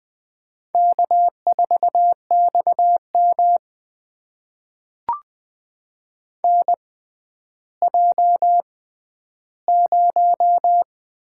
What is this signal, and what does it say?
Morse code, heard as a keyed tone, about 20 wpm, spaced as for 11 wpm, 700 Hz. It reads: K4XM NJ0
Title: USA Call Signs - 4 Characters - 6x Character Spacing - 20wpm